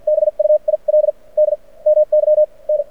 Morse code, heard as a keyed tone, about 53 wpm, spaced as for 37 wpm, 610 Hz. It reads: BUILD MY N
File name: archived / CRSnet